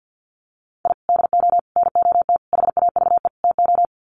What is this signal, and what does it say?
Morse code, heard as a keyed tone, about 50 wpm, 700 Hz. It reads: I DO NOT HAVE TO
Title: Sets of 5 Words 50wpm